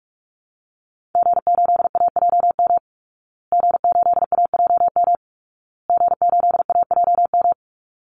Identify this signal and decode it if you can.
Morse code, heard as a keyed tone, about 44 wpm, 700 Hz. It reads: G8AJM G8AJM G8AJM
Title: Call Signs Repeated 2x - 45wpm